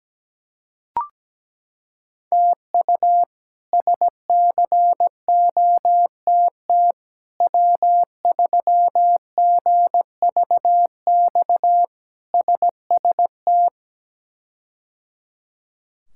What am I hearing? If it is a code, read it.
Morse code, heard as a keyed tone, about 17 wpm, 700 Hz. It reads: TU SCOTT W3GVX SST
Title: K1USN SST QSO Practice - 17wpm